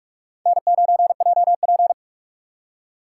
Morse code, heard as a keyed tone, about 45 wpm, 700 Hz. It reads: N9JP